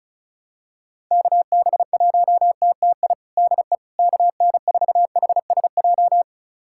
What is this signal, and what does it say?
Morse code, heard as a keyed tone, about 36 wpm, 700 Hz. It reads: KB1TTI DE KN4HSJ
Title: USA Callsign DE Callsign 35wpm